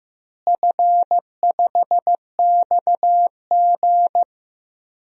Morse code, heard as a keyed tone, about 15 wpm, 700 Hz. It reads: F5XG
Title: DXCC-100 Call Signs - 4 Characters - Repeated 3x - 15wpm